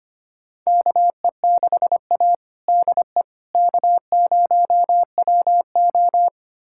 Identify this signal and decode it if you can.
Morse code, heard as a keyed tone, about 25 wpm, 700 Hz. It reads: KE6A DE K0WO
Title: USA Callsign DE Callsign - 4 Characters - 25wpm